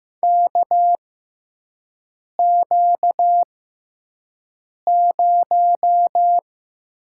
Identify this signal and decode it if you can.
Morse code, heard as a keyed tone, about 15 wpm, spaced as for 5 wpm, 700 Hz. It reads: KQ0
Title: USA Call Signs - 6x Character Spacing - 15wpm